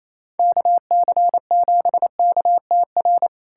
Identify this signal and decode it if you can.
Morse code, heard as a keyed tone, about 28 wpm, 700 Hz. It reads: KC7KTR
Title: USA Callsign DE Callsign 28wpm